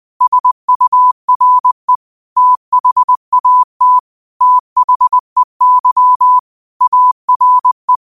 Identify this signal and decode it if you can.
Morse code, heard as a keyed tone, about 20 wpm, 1000 Hz. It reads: SURE THAT THEY ARE
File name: 20wpm